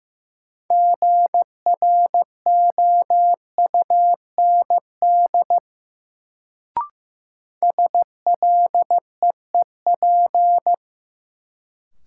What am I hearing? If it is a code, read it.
Morse code, heard as a keyed tone, about 16 wpm, 700 Hz. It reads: GROUND SLEEP
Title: Sentences from Top 500 Words - New Word Drill - 15wpm